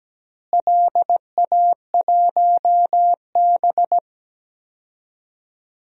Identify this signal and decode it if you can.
Morse code, heard as a keyed tone, about 17 wpm, 700 Hz. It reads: LA1B
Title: DXCC-100 Call Signs - 4 Characters - Repeated 2x - 17wpm